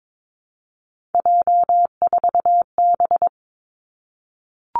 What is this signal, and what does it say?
Morse code, heard as a keyed tone, about 22 wpm, 700 Hz. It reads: J4B
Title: Three Challenging Letter-Number Combinations 22wpm